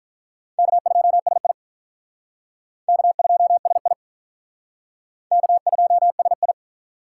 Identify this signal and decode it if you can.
Morse code, heard as a keyed tone, about 45 wpm, 700 Hz. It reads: K2SI K2SI K2SI